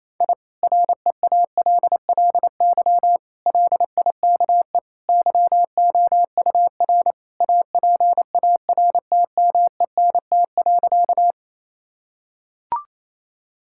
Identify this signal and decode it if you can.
Morse code, heard as a keyed tone, about 28 wpm, 700 Hz. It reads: I REALLY LIKE YOUR APARTMENT.
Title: Sentences from Top 1000 Words - Review Part 6 - 28wpm